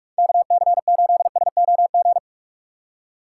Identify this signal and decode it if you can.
Morse code, heard as a keyed tone, about 46 wpm, 700 Hz. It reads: KC8SOG
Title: USA Call Signs - 6 Characters - 2x Character Spacing - 45wpm